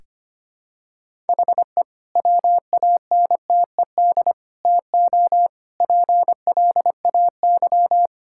Morse code, heard as a keyed tone, about 25 wpm, 700 Hz. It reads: HE WANTED TO PLAY
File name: practice / 25wpm / Sentences with 3-Word Phrases - 25wpm